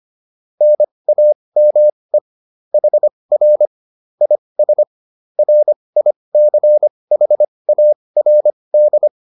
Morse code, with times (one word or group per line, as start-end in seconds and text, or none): 0.6-2.2 NAME
2.7-3.7 HR
4.2-4.8 IS
5.4-9.1 RICHARD